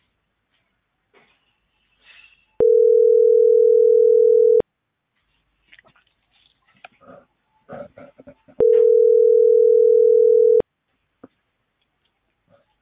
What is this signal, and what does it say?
Ringback tone (North America), 440 + 480 Hz.